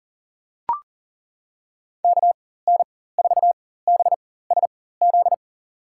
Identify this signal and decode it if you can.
Morse code, heard as a keyed tone, about 40 wpm, spaced as for 19 wpm, 700 Hz. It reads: KN4BSZ
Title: USA Call Signs - 6 Characters - 4x Character Spacing - 40wpm